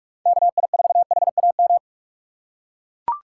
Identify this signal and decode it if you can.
Morse code, heard as a keyed tone, about 41 wpm, 700 Hz. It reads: KI4HAM
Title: USA Call Signs - 6x Character Spacing - 45wpm